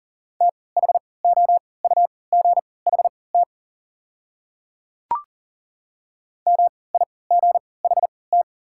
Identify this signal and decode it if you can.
Morse code, heard as a keyed tone, about 40 wpm, spaced as for 23 wpm, 700 Hz. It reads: THOUGHT MIGHT
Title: 3x Character Spacing - Top 300 Words - 40wpm